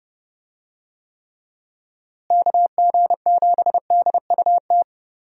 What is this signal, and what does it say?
Morse code, heard as a keyed tone, about 30 wpm, 700 Hz. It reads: KG7DUT
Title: USA Call Signs - 6 Characters - Repeated 3x - 30wpm